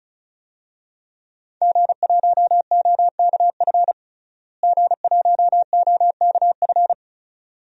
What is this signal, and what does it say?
Morse code, heard as a keyed tone, about 36 wpm, 700 Hz. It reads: G1OKF G1OKF
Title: DXCC-100 Call Signs - 5 Characters - Repeated 2x - 35wpm